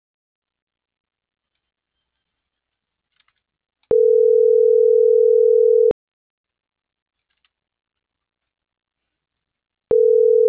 Ringback tone (North America), 440 + 480 Hz.
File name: CENTRAL MAERSK / out adolfo.msk - .27995